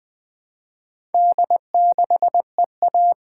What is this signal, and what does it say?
Morse code, heard as a keyed tone, about 20 wpm, 700 Hz. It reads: D6EA
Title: Four Letter-Number Combinations - 20wpm